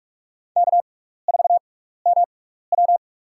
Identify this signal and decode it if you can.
Morse code, heard as a keyed tone, about 47 wpm, spaced as for 16 wpm, 700 Hz. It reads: K4MW
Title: USA Call Signs - 4 Characters - 6x Character Spacing - 45wpm